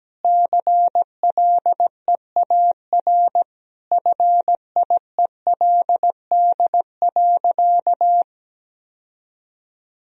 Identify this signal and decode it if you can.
Morse code, heard as a keyed tone, about 17 wpm, 700 Hz. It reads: CLEAR FIELD.
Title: Sentences from Top 1000 Words - Review Part 2 - 17wpm